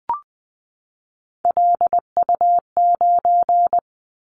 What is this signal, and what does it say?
Morse code, heard as a keyed tone, about 20 wpm, 700 Hz. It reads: LU9
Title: Callsign Prefixes for Top 100 DXCC - 20wpm